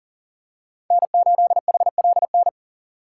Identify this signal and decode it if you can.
Morse code, heard as a keyed tone, about 37 wpm, 700 Hz. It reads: N8HLN
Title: USA Call Signs - 3x Character Spacing - 40wpm